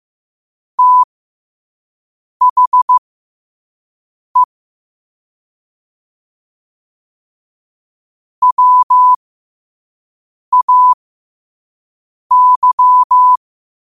Morse code, heard as a keyed tone, about 15 wpm, spaced as for 5 wpm, 1000 Hz. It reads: THE WAY
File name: qotdfiles / 5wpm